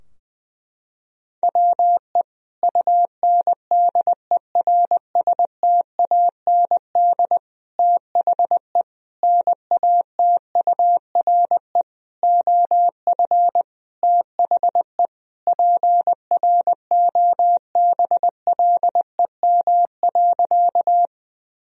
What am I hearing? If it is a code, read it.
Morse code, heard as a keyed tone, about 20 wpm, 700 Hz. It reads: WE UNDERSTAND THE NATURE OF THE PROBLEM.
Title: Sentences from Top 1000 Words - Review Part 2 - 20wpm